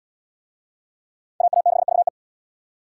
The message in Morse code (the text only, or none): AA6VE